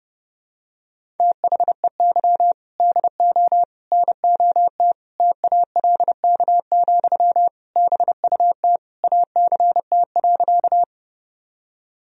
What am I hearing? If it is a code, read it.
Morse code, heard as a keyed tone, about 30 wpm, 700 Hz. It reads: THEY DO NOT TALK, BUT ACT.
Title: Sentences from Top 1000 Words - Review Part 1 - 30wpm